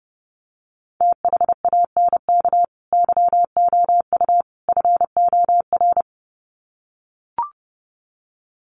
Morse code, heard as a keyed tone, about 28 wpm, 700 Hz. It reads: THANK YOU FOR
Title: Sets of 3 Words 30wpm